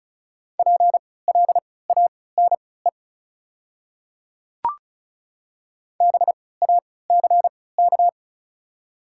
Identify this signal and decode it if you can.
Morse code, heard as a keyed tone, about 34 wpm, spaced as for 20 wpm, 700 Hz. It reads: PLANE BACK
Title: 3x Character Spacing - Top 500 Words - 35wpm